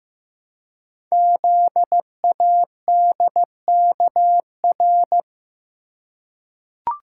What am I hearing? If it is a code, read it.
Morse code, heard as a keyed tone, about 15 wpm, 700 Hz. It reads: ZADKR
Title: Code Groups - 5 Letters - 15wpm